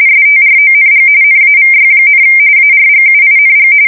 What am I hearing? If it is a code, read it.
RTTY, about 45.45 baud. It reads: STRONG NAVY, ARGUING
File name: fsk170 / usnavy1